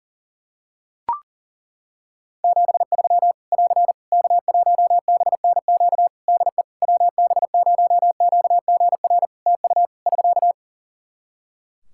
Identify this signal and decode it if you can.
Morse code, heard as a keyed tone, about 40 wpm, 700 Hz. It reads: 73 <AR> K1BNQ DE WB0QGR TU <SK>
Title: Practice QSO - Exchange 4 - Ending - 40wpm